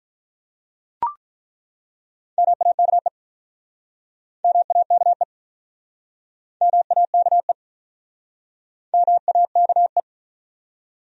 Morse code, heard as a keyed tone, about 50 wpm, 700 Hz. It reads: MAKE MAKE MAKE MAKE